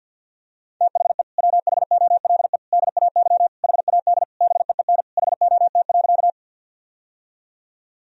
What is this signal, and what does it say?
Morse code, heard as a keyed tone, about 50 wpm, 700 Hz. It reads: THE WHOLE DAY HAD BEEN HOT.
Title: Sentences from Top 1000 Words - Review Part 3 - 50wpm